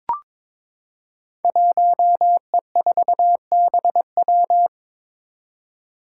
Morse code, heard as a keyed tone, about 23 wpm, 700 Hz. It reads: 1E4BW
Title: Five Challenging Letter-Number Combinations 22wpm